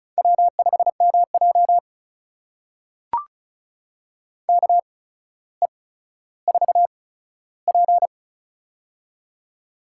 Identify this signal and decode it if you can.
Morse code, heard as a keyed tone, about 36 wpm, spaced as for 15 wpm, 700 Hz. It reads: W5MJ KE4P
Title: USA Call Signs - 4 Characters - 8x Character Spacing - 35wpm